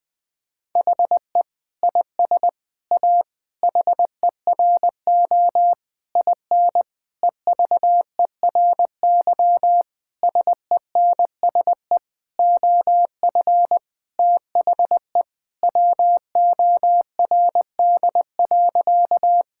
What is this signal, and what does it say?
Morse code, heard as a keyed tone, about 21 wpm, 700 Hz. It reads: HE IS A HERO IN EVERY SENSE OF THE WORD.
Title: Sentences with 6-Word Phrases - 20wpm